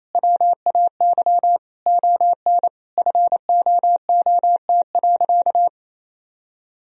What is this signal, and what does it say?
Morse code, heard as a keyed tone, about 27 wpm, 700 Hz. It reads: WAY ON FOOT.
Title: Sentences from Top 1000 Words - Review Part 3 - 28wpm